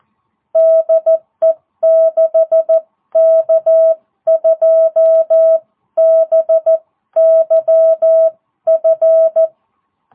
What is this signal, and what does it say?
Morse code, heard as a keyed tone, about 14 wpm, spaced as for 12 wpm, 660 Hz. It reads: DE 6K2BYF